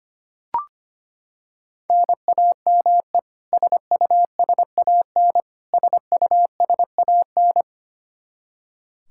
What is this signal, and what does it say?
Morse code, heard as a keyed tone, about 25 wpm, 700 Hz. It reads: NAME SUSAN SUSAN
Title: Practice QSO - Exchange 1 - Beginning - 25wpm